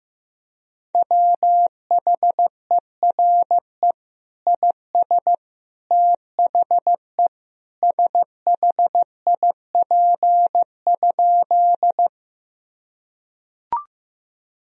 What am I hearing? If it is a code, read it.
Morse code, heard as a keyed tone, about 15 wpm, 700 Hz. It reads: WHERE IS THE SHIP?
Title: Sentences from Top 1000 Words - Review Part 2 - 15wpm